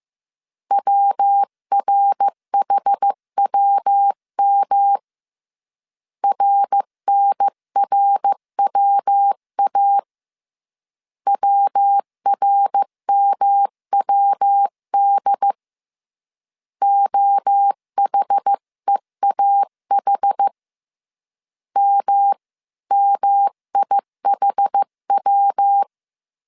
Morse code, heard as a keyed tone, about 15 wpm, 800 Hz. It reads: WRHWM RNRWA WRMWD OHEAH MMIHW
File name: CW / DET / 1 3 1500C S C 14wpm 2458